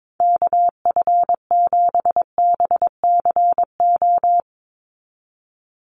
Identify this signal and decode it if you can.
Morse code, heard as a keyed tone, about 22 wpm, 700 Hz. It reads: KF7BCO
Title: USA Call Signs 22wpm